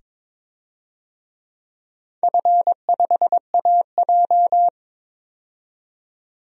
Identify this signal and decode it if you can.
Morse code, heard as a keyed tone, about 22 wpm, 700 Hz. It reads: F5AJ
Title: DXCC-100 Call Signs - 4 Characters - 22wpm